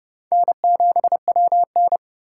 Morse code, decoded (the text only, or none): N7WN